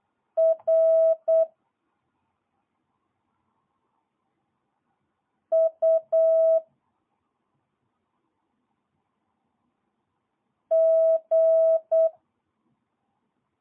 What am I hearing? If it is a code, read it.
Morse code, heard as a keyed tone, about 8 wpm, 650 Hz. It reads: R U G
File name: jh8cbh / 2022cw